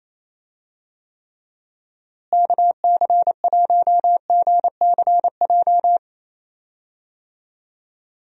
Morse code, heard as a keyed tone, about 28 wpm, 700 Hz. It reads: KC1GCJ
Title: USA Call Signs - 6 Characters - Repeated 3x - 28wpm